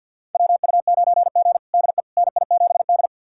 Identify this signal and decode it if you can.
Morse code, heard as a keyed tone, about 51 wpm, 700 Hz. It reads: WU9G DE NI7D
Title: USA Callsign DE Callsign - 4 Characters - 50wpm